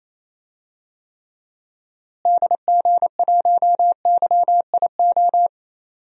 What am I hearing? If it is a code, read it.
Morse code, heard as a keyed tone, about 29 wpm, 700 Hz. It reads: DG1YIO
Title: DXCC-100 Call Signs - 6 Characters - Repeated 2x - 28wpm